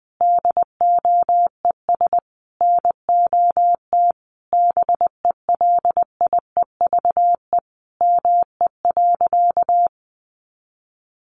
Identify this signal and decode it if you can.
Morse code, heard as a keyed tone, about 20 wpm, 700 Hz. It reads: DOES NOT BELIEVE ME.
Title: Sentences from Top 1000 Words 20wpm